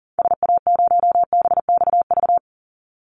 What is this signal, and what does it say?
Morse code, heard as a keyed tone, about 40 wpm, 700 Hz. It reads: SA0BXV
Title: Call Signs Repeated 3x - 40wpm